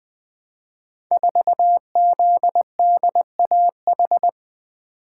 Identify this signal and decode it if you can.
Morse code, heard as a keyed tone, about 20 wpm, 700 Hz. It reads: 4ZDAH
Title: Code Groups - 5 Letter:Numbers - 20wpm